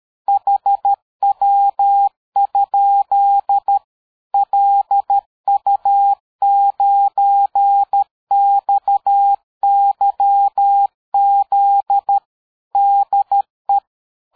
Morse code, heard as a keyed tone, about 13 wpm, 800 Hz. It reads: HW? LU9XYZ DE
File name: dk5ke / QSO 60BpM